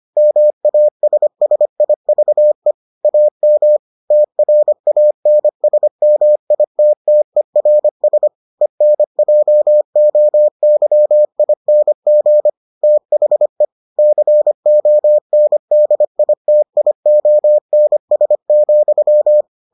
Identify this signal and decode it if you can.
Morse code, heard as a keyed tone, about 25 wpm, 600 Hz. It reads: MASSIVE AM TRANSMITTERS ENJOYING THE CONDITIONS,